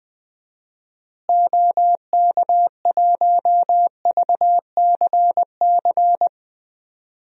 Morse code, heard as a keyed tone, about 20 wpm, 700 Hz. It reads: OK1VCC
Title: DXCC-100 Call Signs - 6 Characters - Repeated 2x - 20wpm